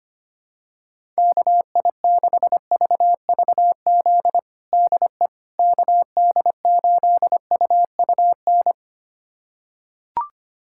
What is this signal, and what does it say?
Morse code, heard as a keyed tone, about 25 wpm, 700 Hz. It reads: KI6VVZ DE KD8UUN
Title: USA Callsign DE Callsign - 6 Characters - 25wpm